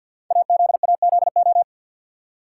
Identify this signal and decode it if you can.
Morse code, heard as a keyed tone, about 48 wpm, 700 Hz. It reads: AZAZO
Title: Code Groups - 5 Letters - 50wpm